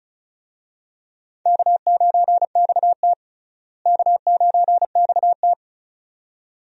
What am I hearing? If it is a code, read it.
Morse code, heard as a keyed tone, about 36 wpm, 700 Hz. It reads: K9XT K9XT
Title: USA Call Signs - 4 Characters - Repeated 2x - 35wpm